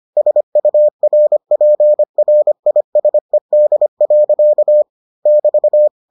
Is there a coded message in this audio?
Yes: Morse code from 0.2 to 5.9 s, heard as a keyed tone, about 25 wpm, 600 Hz. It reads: SURPRISED. <BT>